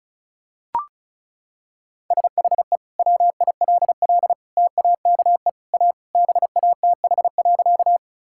Morse code, heard as a keyed tone, about 34 wpm, 700 Hz. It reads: SHE WILL TAKE A BATH.
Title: Sentences from Top 1600 Words 35wpm